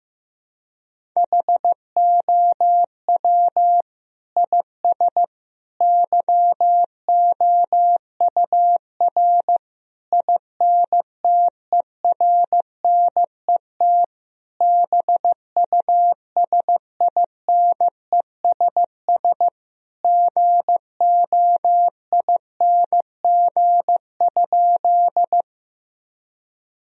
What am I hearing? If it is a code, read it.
Morse code, heard as a keyed tone, about 15 wpm, 700 Hz. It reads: HOW IS YOUR INTERNET BUSINESS GOING?
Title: Sentences from Top 1000 Words - Review Part 4 - 15wpm